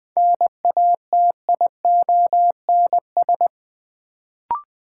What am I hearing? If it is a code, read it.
Morse code, heard as a keyed tone, about 20 wpm, 700 Hz. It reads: NATIONS
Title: Sets of 3 Words - New Word Drill - 20wpm